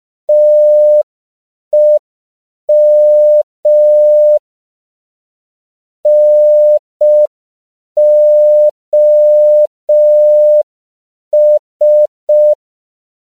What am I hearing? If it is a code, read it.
Morse code, heard as a keyed tone, about 5 wpm, 600 Hz. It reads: TEM NOS
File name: CW / 5PPM